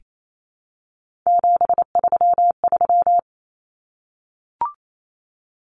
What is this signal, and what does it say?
Morse code, heard as a keyed tone, about 28 wpm, 700 Hz. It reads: 733
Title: Common Cut Numbers - 3-Digit 28wpm